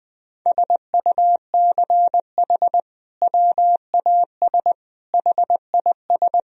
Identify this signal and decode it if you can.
Morse code, heard as a keyed tone, about 21 wpm, 700 Hz. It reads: SUCH WAS HIS
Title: Sentences from Top 1000 Words - Review Part 1 - 20wpm